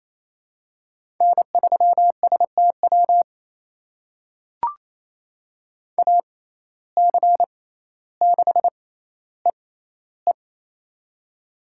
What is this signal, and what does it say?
Morse code, heard as a keyed tone, about 28 wpm, spaced as for 14 wpm, 700 Hz. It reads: N3STW AC6EE
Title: USA Call Signs - 5 Characters - 6x Character Spacing - 28wpm